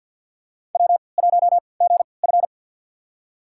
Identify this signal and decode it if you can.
Morse code, heard as a keyed tone, about 46 wpm, spaced as for 27 wpm, 700 Hz. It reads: W1GF
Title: USA Call Signs - 3x Character Spacing - 50wpm